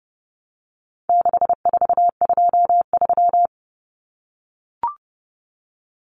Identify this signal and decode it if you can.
Morse code, heard as a keyed tone, about 30 wpm, 700 Hz. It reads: 6423
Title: Numbers - 4-Digit 30wpm